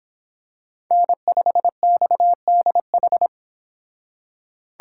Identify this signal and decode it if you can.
Morse code, heard as a keyed tone, about 26 wpm, 700 Hz. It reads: N5XDH